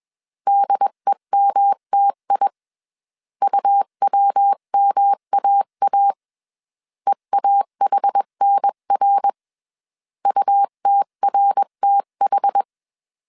Morse code, heard as a keyed tone, about 21 wpm, 800 Hz. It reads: DEMTI UWMAA EAHNR UTRTH
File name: 20wpm